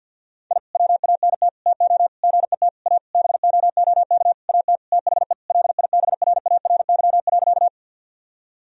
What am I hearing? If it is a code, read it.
Morse code, heard as a keyed tone, about 52 wpm, 700 Hz. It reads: I WANT TO GET A BOOK AT THE LIBRARY.
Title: Sentences from Top 1000 Words - Review Part 1 - 50wpm